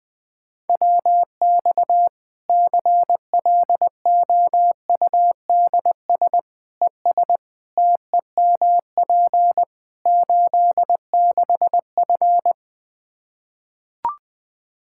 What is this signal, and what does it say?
Morse code, heard as a keyed tone, about 20 wpm, 700 Hz. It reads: WX CLOUDS ES TEMP 86F
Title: Practice QSO - Exchange 2 - Info - 20wpm